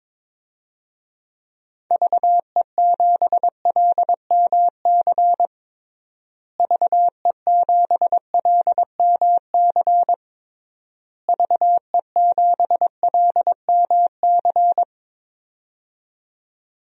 Morse code, heard as a keyed tone, about 21 wpm, 700 Hz. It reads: VE7LMC VE7LMC VE7LMC
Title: DXCC-100 Call Signs - 6 Characters - Repeated 3x - 22wpm